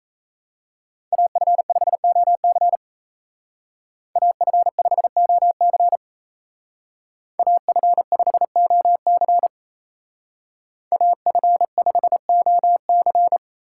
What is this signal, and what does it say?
Morse code, heard as a keyed tone, about 33 wpm, 700 Hz. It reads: AF5OC AF5OC AF5OC AF5OC
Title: Speed Racing - USA 2x2 Call Signs 28wpm